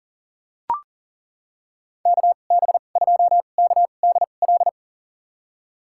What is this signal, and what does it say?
Morse code, heard as a keyed tone, about 40 wpm, spaced as for 29 wpm, 700 Hz. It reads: KB2KDL
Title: USA Call Signs - 6 Characters - 2x Character Spacing - 40wpm